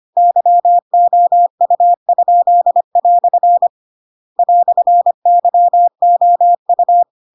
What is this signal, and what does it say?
Morse code, heard as a keyed tone, about 25 wpm, 700 Hz. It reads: YOU?" "YOU
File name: Pinocchio / Pinocchio ch2 25 25 ik1hge